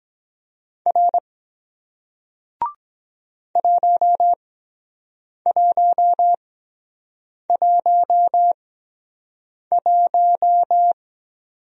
Morse code, heard as a keyed tone, about 23 wpm, spaced as for 7 wpm, 700 Hz. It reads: R 1111